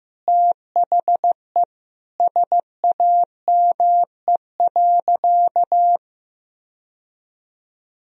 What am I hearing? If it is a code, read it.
Morse code, heard as a keyed tone, about 15 wpm, 700 Hz. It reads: THE SAME.
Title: Sentences from Top 1000 Words - Review Part 1 - 15wpm